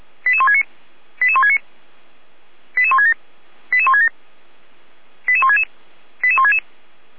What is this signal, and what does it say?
Five-tone selective call (ZVEI), heard as quick runs of five tones, tones of about 80 ms. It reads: 88169 88169 88167 88167 88166 88166